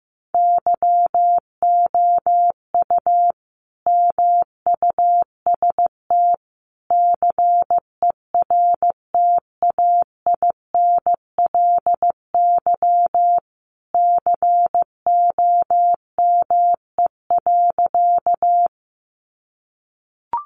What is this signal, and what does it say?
Morse code, heard as a keyed tone, about 15 wpm, 700 Hz. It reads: YOU MUST CERTAINLY COME.
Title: Sentences from Top 900 Words 15wpm